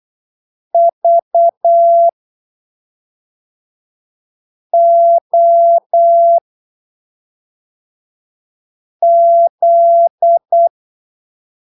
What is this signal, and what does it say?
Morse code, heard as a keyed tone, about 8 wpm, spaced as for 3 wpm, 680 Hz. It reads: VOZ